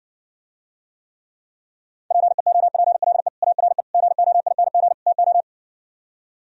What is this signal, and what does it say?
Morse code, heard as a keyed tone, about 68 wpm, 700 Hz. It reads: PEOPLE ARE GOING TO